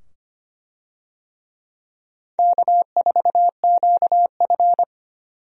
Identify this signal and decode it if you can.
Morse code, heard as a keyed tone, about 25 wpm, 700 Hz. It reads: K4QF
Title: USA Call Signs - 4 Characters - 25wpm